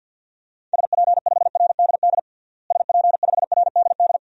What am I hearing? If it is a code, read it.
Morse code, heard as a keyed tone, about 47 wpm, 700 Hz. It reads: SP5RDD SP5RDD
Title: DXCC-100 Call Signs - 6 Characters - Repeated 2x - 50wpm